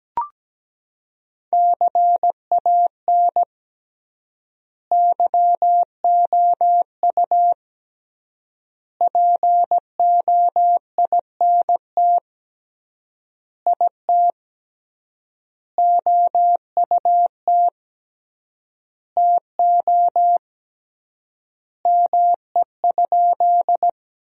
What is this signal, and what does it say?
Morse code, heard as a keyed tone, about 17 wpm, 700 Hz. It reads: CAN YOU POINT IT OUT TO ME?
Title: 3x Word-Spacing - Sentences from Top 200 Words - 17wpm